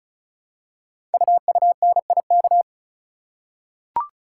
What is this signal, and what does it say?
Morse code, heard as a keyed tone, about 36 wpm, 700 Hz. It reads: UUNIK
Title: Code Groups - 5 Letters - 35wpm